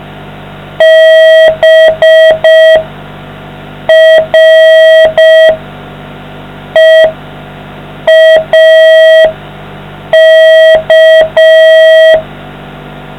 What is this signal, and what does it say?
Morse code, heard as a keyed tone, about 5 wpm, spaced as for 4 wpm, 640 Hz. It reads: BREAK